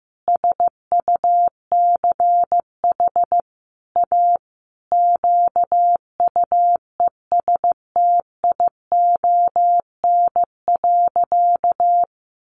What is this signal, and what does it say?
Morse code, heard as a keyed tone, about 15 wpm, 700 Hz. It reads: SUCH A QUESTION.
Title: Sentences from Top 600 Words 15wpm